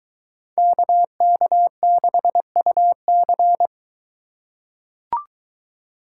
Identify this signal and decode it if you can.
Morse code, heard as a keyed tone, about 22 wpm, 700 Hz. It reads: KK6UC